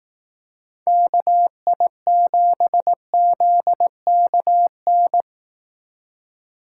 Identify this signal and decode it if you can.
Morse code, heard as a keyed tone, about 18 wpm, 700 Hz. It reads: KI7ZKN
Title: Speed Racing - USA Call Signs 15wpm